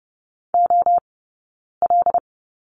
Morse code, heard as a keyed tone, about 30 wpm, spaced as for 9 wpm, 700 Hz. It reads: OL
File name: Lesson 18d - Letter m - Two Characters 30wpm